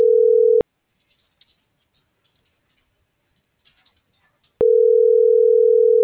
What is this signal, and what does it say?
Ringback tone (North America), 440 + 480 Hz.